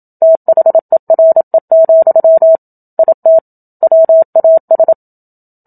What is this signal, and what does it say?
Morse code, heard as a keyed tone, about 27 wpm, 650 Hz. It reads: THERE, IT WAS
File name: Book Two / 01-Chapter 1 - 27 WPM